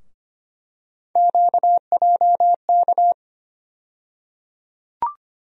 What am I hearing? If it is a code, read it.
Morse code, heard as a keyed tone, about 25 wpm, 700 Hz. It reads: QJK
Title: USA Call Sign Postfixes - 3 Letters 25wpm